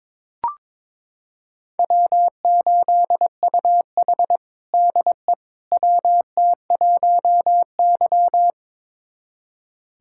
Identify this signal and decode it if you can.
Morse code, heard as a keyed tone, about 21 wpm, 700 Hz. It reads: W8UH DE WT1Y
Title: USA Callsign DE Callsign - 4 Characters - 22wpm